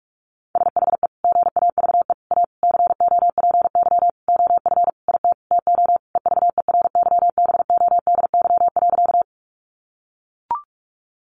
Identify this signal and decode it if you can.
Morse code, heard as a keyed tone, about 44 wpm, 700 Hz. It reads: SHE GAVE A COPY OF IT TO EVERYBODY.
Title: Sentences from Top 1400 Words 45wpm